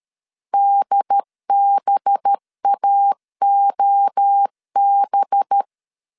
Morse code, heard as a keyed tone, about 13 wpm, 800 Hz. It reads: DBAOB
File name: CW / DET / 1 6 1500C S A 12wpm 3440